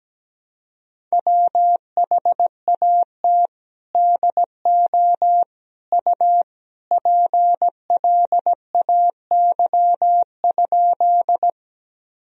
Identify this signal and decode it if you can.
Morse code, heard as a keyed tone, about 17 wpm, 700 Hz. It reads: WHAT DO U PLAY?
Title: Sentences from Top 200 Words and Abbreviations 17wpm